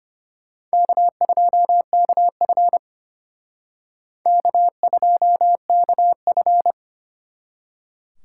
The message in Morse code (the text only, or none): K2KF K2KF